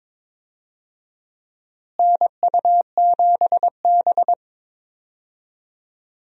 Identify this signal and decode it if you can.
Morse code, heard as a keyed tone, about 21 wpm, 700 Hz. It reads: NU7B
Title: USA Call Signs - 4 Characters - Repeated 3x - 22wpm